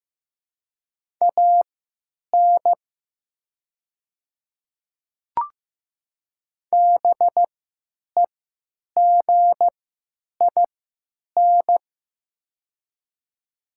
Morse code, heard as a keyed tone, about 15 wpm, spaced as for 9 wpm, 700 Hz. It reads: AN BEGIN
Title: 3x Character Spacing - Top 500 Words - 15wpm